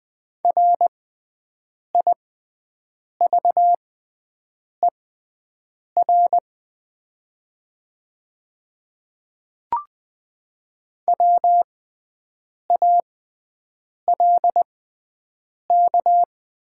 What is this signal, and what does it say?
Morse code, heard as a keyed tone, about 20 wpm, spaced as for 7 wpm, 690 Hz. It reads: RIVER WALK
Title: 6x Character Spacing - Top 300 Words - 20wpm